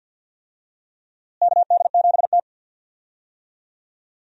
Morse code, heard as a keyed tone, about 50 wpm, 700 Hz. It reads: KD7T
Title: USA Call Signs - 4 Characters - Repeated 2x - 50wpm